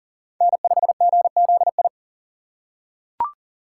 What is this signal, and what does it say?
Morse code, heard as a keyed tone, about 40 wpm, 700 Hz. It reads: N5GZI
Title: USA Call Signs - 2x Character Spacing - 40wpm